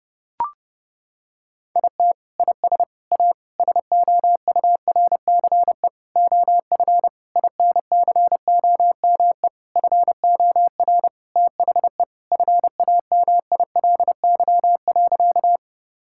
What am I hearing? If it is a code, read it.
Morse code, heard as a keyed tone, about 30 wpm, 700 Hz. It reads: IT IS A SOURCE OF INCOME FOR THE FAMILY.
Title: Sentences from Top 1000 Words - Review Part 5 - 30wpm